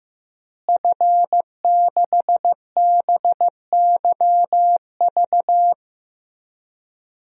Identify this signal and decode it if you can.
Morse code, heard as a keyed tone, about 15 wpm, 700 Hz. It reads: F6BYV
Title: DXCC-100 Call Signs - 5 Characters - Repeated 3x - 15wpm